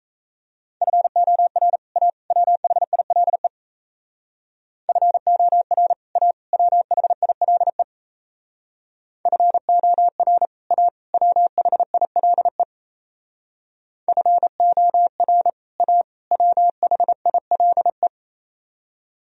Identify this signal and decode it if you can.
Morse code, heard as a keyed tone, about 33 wpm, 700 Hz. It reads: FOR A WHILE FOR A WHILE FOR A WHILE FOR A WHILE
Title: Speed-Racing - Sets of 3 Words 28wpm